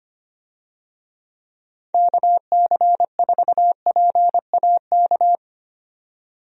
Morse code, heard as a keyed tone, about 25 wpm, 700 Hz. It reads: KC4PAK
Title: USA Call Signs - 6 Characters - 25wpm